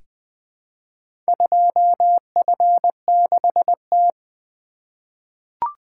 Morse code, heard as a keyed tone, about 20 wpm, 700 Hz. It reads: 2F6T
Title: Four Letter-Number Combinations - 20wpm